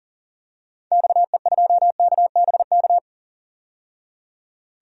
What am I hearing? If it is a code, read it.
Morse code, heard as a keyed tone, about 40 wpm, 700 Hz. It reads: XE2KBK
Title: DXCC-100 Call Signs - 6 Characters - 40wpm